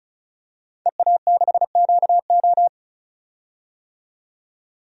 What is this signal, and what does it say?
Morse code, heard as a keyed tone, about 36 wpm, 700 Hz. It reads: EA6QO
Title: DXCC-100 Call Signs - 5 Characters - 35wpm